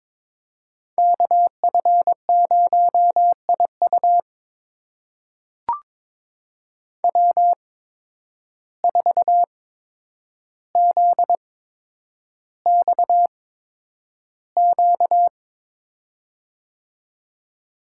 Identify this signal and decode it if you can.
Morse code, heard as a keyed tone, about 22 wpm, spaced as for 9 wpm, 700 Hz. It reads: KF0IU W4ZXQ
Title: USA Call Signs - 5 Characters - 8x Character Spacing - 22wpm